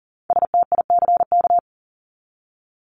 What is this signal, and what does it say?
Morse code, heard as a keyed tone, about 40 wpm, 700 Hz. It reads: STICK
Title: Top 1001-1250 Words 40wpm